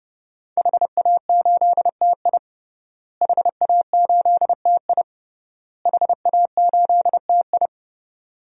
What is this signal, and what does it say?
Morse code, heard as a keyed tone, about 31 wpm, 700 Hz. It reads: HA8TI HA8TI HA8TI
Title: DXCC-100 Call Signs - 5 Characters - Repeated 3x - 30wpm